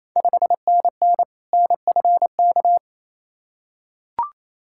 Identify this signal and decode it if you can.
Morse code, heard as a keyed tone, about 28 wpm, 690 Hz. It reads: 5NN NFK
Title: Ontario QSO Party Exchange 28wpm